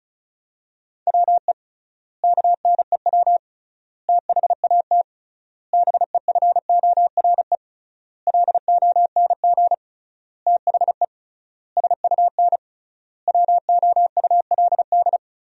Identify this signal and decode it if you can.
Morse code, heard as a keyed tone, about 36 wpm, 700 Hz. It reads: WE KNEW THAT BEFORE LONG THE SUN WOULD
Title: 3x Word-Spacing - Sentences from Top 400 Words - 35wpm